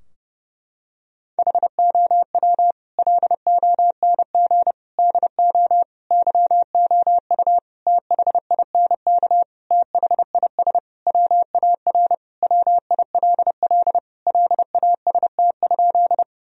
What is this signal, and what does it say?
Morse code, heard as a keyed tone, about 29 wpm, 700 Hz. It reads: HOW LONG DO YOU THINK THIS WAR WILL LAST?
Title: Sentences from Top 1000 Words - Review Part 2 - 30wpm